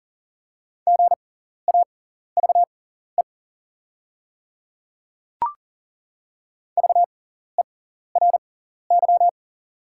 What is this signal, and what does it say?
Morse code, heard as a keyed tone, about 39 wpm, spaced as for 14 wpm, 690 Hz. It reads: GAVE VERY
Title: 6x Character Spacing - Top 500 Words - 40wpm